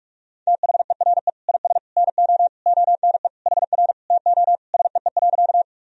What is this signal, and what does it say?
Morse code, heard as a keyed tone, about 47 wpm, 700 Hz. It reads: THERE IS NO ONE HR TO SEE.
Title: Sentences from Top 200 Words and Abbreviations 45wpm